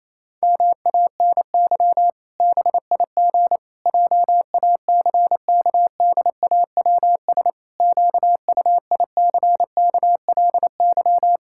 Morse code, heard as a keyed tone, about 28 wpm, 700 Hz. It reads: MANY BIG JACKDAWS QUICKLY
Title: Pangram Challenge v2 - 28wpm